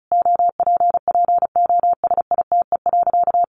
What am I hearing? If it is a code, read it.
Morse code, heard as a keyed tone, about 34 wpm, 700 Hz. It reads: OPPOSITE.
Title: Sentences from Top 1300 Words 35wpm